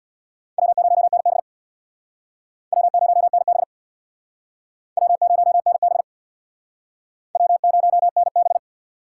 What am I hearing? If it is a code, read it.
Morse code, heard as a keyed tone, about 64 wpm, 700 Hz. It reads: W0NB W0NB W0NB W0NB